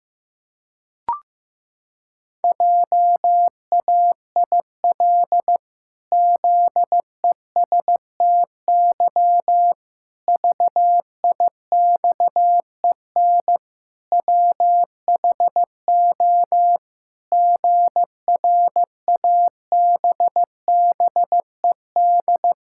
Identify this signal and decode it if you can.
Morse code, heard as a keyed tone, about 15 wpm, 700 Hz. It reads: JAIL ZESTY VIXEN WHO GRABBED
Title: Pangram Challenge v2 - 15wpm